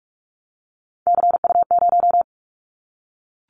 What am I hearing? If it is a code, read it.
Morse code, heard as a keyed tone, about 44 wpm, 700 Hz. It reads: CU0
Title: Callsign Prefixes for Top 50 DXCC - 45wpm